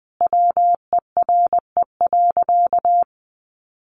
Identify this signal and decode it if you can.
Morse code, heard as a keyed tone, about 20 wpm, 700 Hz. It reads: WERE.
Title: Sentences from Top 900 Words 20wpm